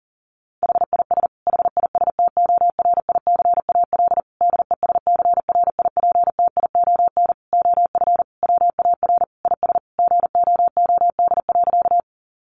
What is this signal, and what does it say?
Morse code, heard as a keyed tone, about 40 wpm, 700 Hz. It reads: HIS HISTORICAL DESCRIPTION OF WAR IS GOOD.
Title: Sentences from Top 1300 Words 40wpm